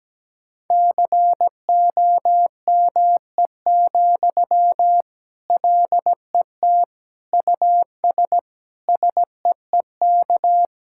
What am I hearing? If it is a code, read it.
Morse code, heard as a keyed tone, about 17 wpm, 700 Hz. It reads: COME, LET US SEEK
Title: Sentences from Top 1600 Words 17wpm